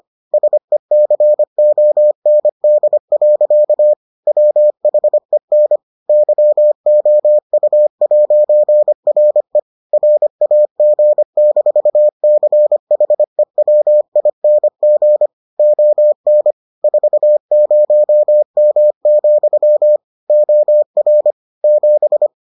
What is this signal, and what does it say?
Morse code, heard as a keyed tone, about 25 wpm, 600 Hz. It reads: SECOND. WHEN YOU'RE RAG-CHEWING ON 40M, OR 7